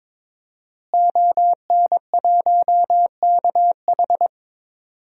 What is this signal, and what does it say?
Morse code, heard as a keyed tone, about 22 wpm, 700 Hz. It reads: ON1KH
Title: DXCC-100 Call Signs - 5 Characters - Repeated 3x - 22wpm